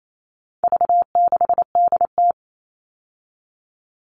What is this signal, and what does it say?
Morse code, heard as a keyed tone, about 28 wpm, 700 Hz. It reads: V6DT
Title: All Cut Numbers - 4-Digit 28wpm